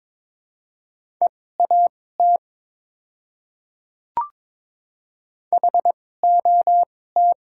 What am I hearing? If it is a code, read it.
Morse code, heard as a keyed tone, about 22 wpm, spaced as for 16 wpm, 700 Hz. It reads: EAT HOT
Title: 2x Character Spacing - Top 500 Words - 22wpm